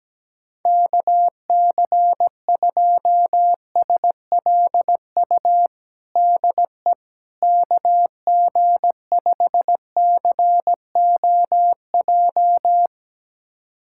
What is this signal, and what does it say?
Morse code, heard as a keyed tone, about 17 wpm, 700 Hz. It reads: KC2SLU DE KG5COJ
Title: USA Callsign DE Callsign - 6 Characters - 17wpm